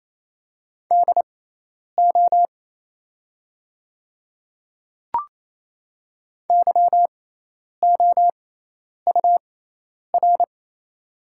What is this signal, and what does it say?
Morse code, heard as a keyed tone, about 28 wpm, spaced as for 10 wpm, 700 Hz. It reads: DO YOUR
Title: 6x Character Spacing - Top 100 Words - 28wpm